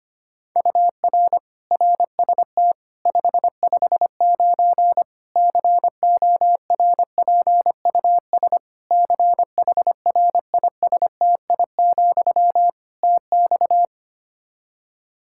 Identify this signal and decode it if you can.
Morse code, heard as a keyed tone, about 25 wpm, 700 Hz. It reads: UR RST 559 CORPUS CHRISTI, TX
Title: Practice QSO - Exchange 1 - Beginning - 25wpm